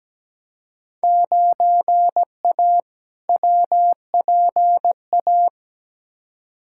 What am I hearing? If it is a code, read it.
Morse code, heard as a keyed tone, about 17 wpm, 700 Hz. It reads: 9A WPA
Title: Field Day Exchanges Encore - 17wpm